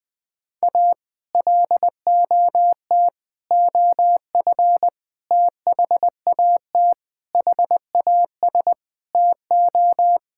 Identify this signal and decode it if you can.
Morse code, heard as a keyed tone, about 20 wpm, 700 Hz. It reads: A LOT OF THAT HAS TO
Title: Sentences with 8-Word Phrases - Spoken Beginning - 20wpm